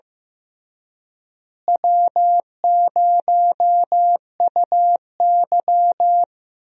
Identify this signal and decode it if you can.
Morse code, heard as a keyed tone, about 15 wpm, 700 Hz. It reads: W0UY
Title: USA Call Signs - 4 Characters - Repeated 3x - 15wpm